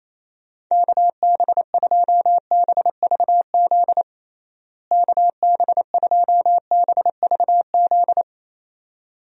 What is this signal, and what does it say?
Morse code, heard as a keyed tone, about 28 wpm, 700 Hz. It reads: KB2BVZ KB2BVZ
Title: USA Call Signs - 6 Characters - Repeated 2x - 28wpm